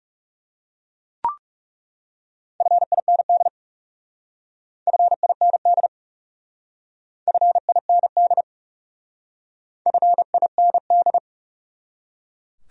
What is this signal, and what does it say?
Morse code, heard as a keyed tone, about 36 wpm, 700 Hz. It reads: FIND FIND FIND FIND
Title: Speed-Racing Top 100 Words 30wpm